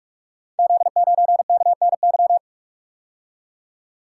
Morse code, heard as a keyed tone, about 44 wpm, 700 Hz. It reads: Z9KNY
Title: Code Groups - 5 Letter:Numbers - 45wpm